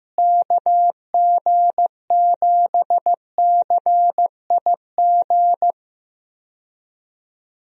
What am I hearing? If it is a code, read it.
Morse code, heard as a keyed tone, about 15 wpm, 700 Hz. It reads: KG7CIG
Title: USA Call Signs - 6 Characters - Repeated 2x - 15wpm